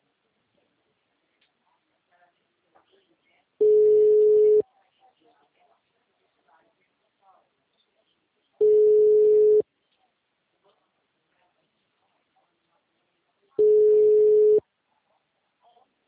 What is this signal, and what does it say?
Ringback tone (Europe, 425 Hz), with speech.